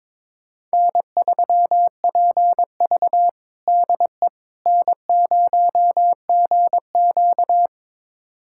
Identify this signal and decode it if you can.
Morse code, heard as a keyed tone, about 22 wpm, 700 Hz. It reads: N3PV DE N0GQ
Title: USA Callsign DE Callsign - 4 Characters - 22wpm